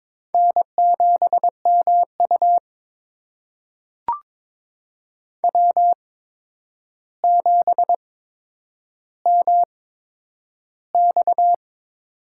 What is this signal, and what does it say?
Morse code, heard as a keyed tone, about 22 wpm, spaced as for 9 wpm, 700 Hz. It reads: N7MU W7MX
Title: USA Call Signs - 4 Characters - 8x Character Spacing - 22wpm